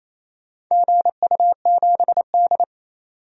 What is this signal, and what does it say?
Morse code, heard as a keyed tone, about 28 wpm, 700 Hz. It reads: GU7D
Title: DXCC-100 Call Signs - 4 Characters - 28wpm